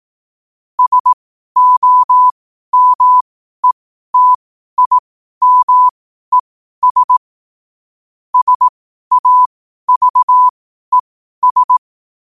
Morse code, heard as a keyed tone, about 18 wpm, spaced as for 12 wpm, 1000 Hz. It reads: SOMETIMES SAVES
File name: qotdfiles / 12wpm